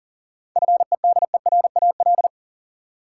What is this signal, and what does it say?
Morse code, heard as a keyed tone, about 40 wpm, 700 Hz. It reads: FEDERAL
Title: Sentences from Top 1000 Words - Review Part 5 - New Word Drill - 40wpm